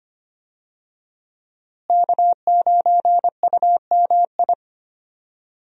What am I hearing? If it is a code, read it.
Morse code, heard as a keyed tone, about 25 wpm, 700 Hz. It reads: K9UMI
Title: USA Call Signs - 5 Characters - Repeated 3x - 25wpm